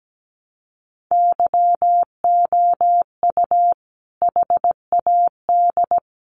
Morse code, heard as a keyed tone, about 17 wpm, 700 Hz. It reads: YOU HAD
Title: Sentences from Top 1200 Words 17wpm